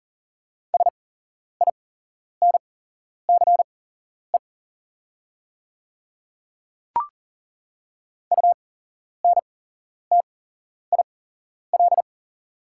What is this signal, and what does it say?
Morse code, heard as a keyed tone, about 40 wpm, spaced as for 11 wpm, 700 Hz. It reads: SINCE UNTIL
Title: 8x Character Spacing - Top 300 Words - 40wpm